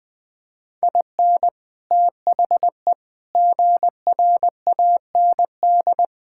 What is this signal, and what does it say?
Morse code, heard as a keyed tone, about 20 wpm, 700 Hz. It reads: IN THE GRAND
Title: Sets of 6 Words - 20wpm